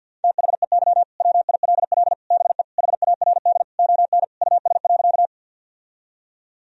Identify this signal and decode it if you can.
Morse code, heard as a keyed tone, about 52 wpm, 700 Hz. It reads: THEY WILL BE HARD ON US.
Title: Sentences from Top 1000 Words - Review Part 1 - 50wpm